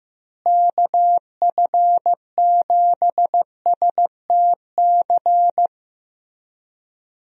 Morse code, heard as a keyed tone, about 15 wpm, 700 Hz. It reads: KF7STC
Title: USA Call Signs - 6 Characters - Repeated 3x - 15wpm